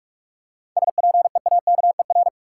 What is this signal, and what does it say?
Morse code, heard as a keyed tone, about 46 wpm, 700 Hz. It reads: SPEAKER
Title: Seven Letter Words - 45wpm